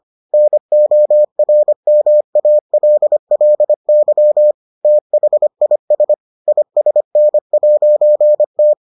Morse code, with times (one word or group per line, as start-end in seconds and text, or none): 0.3-4.5 NORMALLY
4.8-6.1 THIS
6.5-8.7 ISN'T